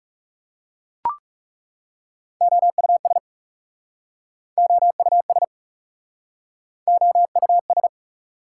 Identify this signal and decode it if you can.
Morse code, heard as a keyed tone, about 37 wpm, 700 Hz. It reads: OUS OUS OUS
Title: Speed-Racing Prefixes and Suffixes 30wpm